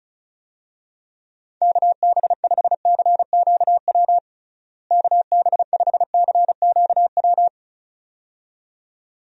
Morse code, heard as a keyed tone, about 36 wpm, 700 Hz. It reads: KB5CQW KB5CQW
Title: USA Call Signs - 6 Characters - Repeated 2x - 35wpm